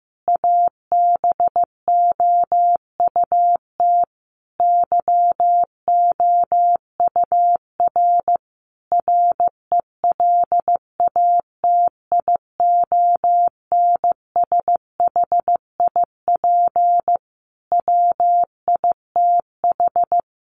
Morse code, heard as a keyed tone, about 15 wpm, 700 Hz. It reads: ABOUT YOUR RELATIONSHIP WITH
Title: Sentences from Top 1100 Words 15wpm